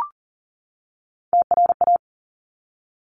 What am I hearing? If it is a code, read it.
Morse code, heard as a keyed tone, about 39 wpm, 700 Hz. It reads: TRA
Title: Top Three Letter Combinations in Words 40wpm